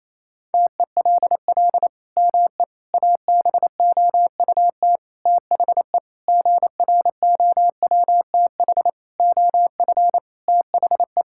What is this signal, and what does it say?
Morse code, heard as a keyed tone, about 28 wpm, 700 Hz. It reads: TELL ME ABOUT THE GROWTH OF THE
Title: Sentences from Top 1000 Words - Review Part 5 - 28wpm